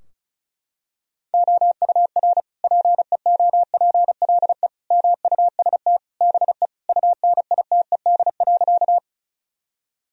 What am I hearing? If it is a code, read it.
Morse code, heard as a keyed tone, about 36 wpm, 700 Hz. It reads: OUR PEOPLE MUST BE UNITED.
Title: Sentences from Top 1000 Words - Review Part 4 - 35wpm